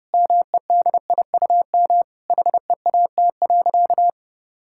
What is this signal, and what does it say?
Morse code, heard as a keyed tone, about 30 wpm, 700 Hz. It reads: MEDIUM HEAT.
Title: Sentences from Top 1000 Words - Review Part 6 - 30wpm